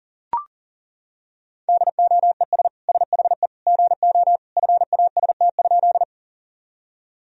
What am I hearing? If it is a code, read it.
Morse code, heard as a keyed tone, about 40 wpm, 700 Hz. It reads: DOES SHE GO FAST?
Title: Sentences from Top 1000 Words - Review Part 2 - 40wpm